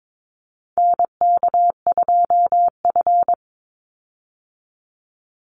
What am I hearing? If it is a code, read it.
Morse code, heard as a keyed tone, about 22 wpm, 700 Hz. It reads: NK2F
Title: Call Signs Repeated 3x - 22wpm